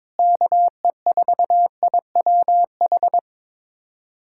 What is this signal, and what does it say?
Morse code, heard as a keyed tone, about 21 wpm, 700 Hz. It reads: KE4IWH
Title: USA Callsign DE Callsign - 6 Characters - 22wpm